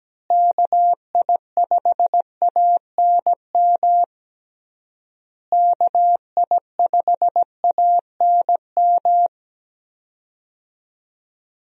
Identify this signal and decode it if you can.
Morse code, heard as a keyed tone, about 17 wpm, 700 Hz. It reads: KI5ANM KI5ANM
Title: USA Call Signs - 6 Characters - Repeated 3x - 17wpm